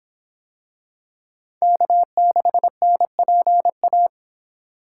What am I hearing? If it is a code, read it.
Morse code, heard as a keyed tone, about 26 wpm, 700 Hz. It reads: K6NPA